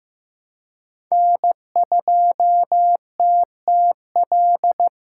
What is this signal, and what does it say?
Morse code, heard as a keyed tone, about 15 wpm, 700 Hz. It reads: N2TTL
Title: USA Call Signs - 5 Characters - 2x Character Spacing - 15wpm